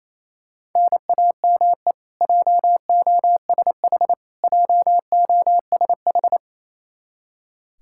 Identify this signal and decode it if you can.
Morse code, heard as a keyed tone, about 28 wpm, 700 Hz. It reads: NAME JOSH JOSH
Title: K1USN SST QSO Practice 28wpm